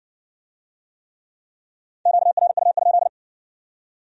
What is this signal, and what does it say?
Morse code, heard as a keyed tone, about 56 wpm, 700 Hz. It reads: QRU?